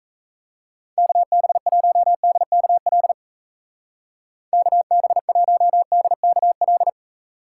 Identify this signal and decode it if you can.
Morse code, heard as a keyed tone, about 36 wpm, 700 Hz. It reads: KB1DKL KB1DKL